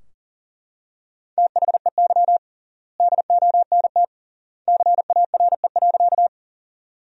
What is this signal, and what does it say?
Morse code, heard as a keyed tone, about 40 wpm, 700 Hz. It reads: THEY DONT CARE.
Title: 3x Word-Spacing - Sentences from Top 300 Words - 40wpm